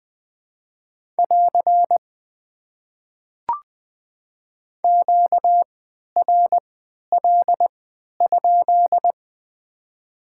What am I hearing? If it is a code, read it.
Morse code, heard as a keyed tone, about 20 wpm, spaced as for 11 wpm, 700 Hz. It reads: <AR> QRL?
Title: Top 90 QSO Elements - 3x Character Spacing - 20wpm